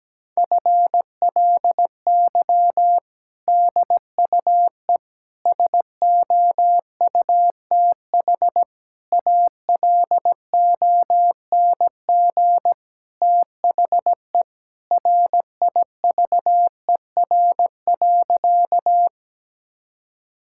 Morse code, heard as a keyed tone, about 17 wpm, 700 Hz. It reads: FLY DUE SOUTH ALONG THE RIVER.
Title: Sentences from Top 1000 Words - Review Part 6 - 17wpm